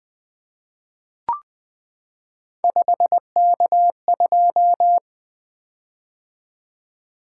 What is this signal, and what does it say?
Morse code, heard as a keyed tone, about 21 wpm, 700 Hz. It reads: 5K2
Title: Three Letter-Number Combinations - 20wpm